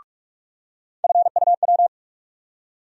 Morse code, heard as a keyed tone, about 42 wpm, 700 Hz. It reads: FUW
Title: USA Call Sign Postfixes - 3 Letters 45wpm